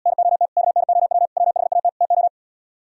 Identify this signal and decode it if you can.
Morse code, heard as a keyed tone, about 78 wpm, 700 Hz. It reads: NOT KNOW WHAT TO